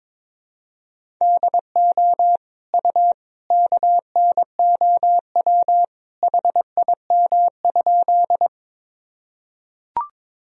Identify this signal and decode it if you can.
Morse code, heard as a keyed tone, about 22 wpm, 700 Hz. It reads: DO U KNOW HIM?
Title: Sentences from Top 100 Words and Abbreviations 22wpm